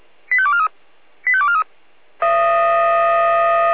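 Five-tone selective call (ZVEI), heard as quick runs of five tones, tones of about 70 ms. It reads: 85243 85243
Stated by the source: ZVEI five-tone selective call received by SDR.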